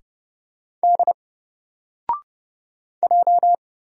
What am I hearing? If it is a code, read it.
Morse code, heard as a keyed tone, about 31 wpm, 700 Hz. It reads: D J
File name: Single-Letter-Number-30wpm